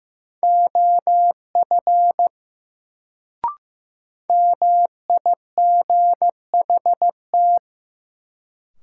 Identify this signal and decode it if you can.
Morse code, heard as a keyed tone, about 15 wpm, 700 Hz. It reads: OF MIGHT
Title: Sets of 2 Words - New Word Drill - 15wpm